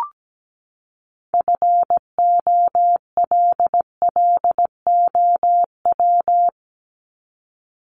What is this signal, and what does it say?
Morse code, heard as a keyed tone, about 17 wpm, 700 Hz. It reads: FOLLOW
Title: ING Suffix - Top 100 Words with Base Words 17wpm